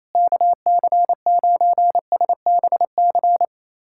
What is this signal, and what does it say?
Morse code, heard as a keyed tone, about 28 wpm, 700 Hz. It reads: KC9SBC
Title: USA Callsign DE Callsign - 6 Characters - 28wpm